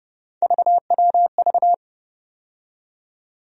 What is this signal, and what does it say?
Morse code, heard as a keyed tone, about 29 wpm, 700 Hz. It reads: VWV
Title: USA Call Sign Postfixes - 3 Letters 30wpm